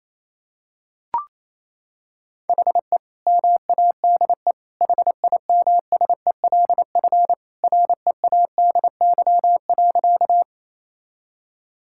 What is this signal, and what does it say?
Morse code, heard as a keyed tone, about 28 wpm, 700 Hz. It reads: HE MADE HIMSELF READY.
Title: Sentences from Top 1600 Words 28wpm